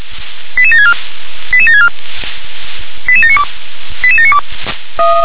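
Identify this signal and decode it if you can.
Five-tone selective call (ZVEI), heard as quick runs of five tones, tones of about 70 ms. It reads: 88764 88764 88792 88792
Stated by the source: ZVEI five-tone selective call received by SDR.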